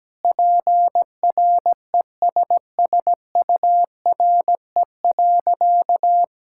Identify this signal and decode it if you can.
Morse code, heard as a keyed tone, about 17 wpm, 700 Hz. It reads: PRESSURE.
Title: Sentences from Top 1000 Words - Review Part 6 - 17wpm